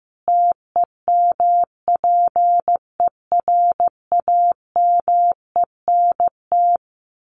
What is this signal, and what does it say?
Morse code, heard as a keyed tone, about 15 wpm, 700 Hz. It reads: TEMPERAMENT
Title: A Sound Of Thunder 15wpm